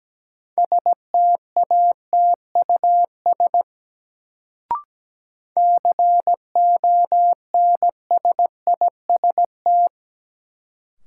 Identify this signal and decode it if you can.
Morse code, heard as a keyed tone, about 17 wpm, 700 Hz. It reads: STATUS CONSIST
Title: Sentences from Top 1400 Words - New Word Drill - 17wpm